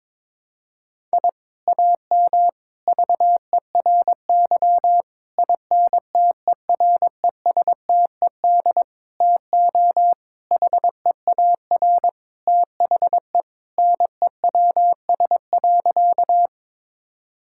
Morse code, heard as a keyed tone, about 22 wpm, 700 Hz. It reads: I AM VERY INTERESTED TO HEAR THE NEWS.
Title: Sentences from Top 1000 Words - Review Part 6 - 22wpm